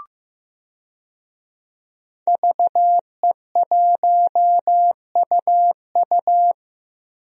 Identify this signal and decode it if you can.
Morse code, heard as a keyed tone, about 15 wpm, 700 Hz. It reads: VE1UU
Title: DXCC-100 Call Signs - 5 Characters - 15wpm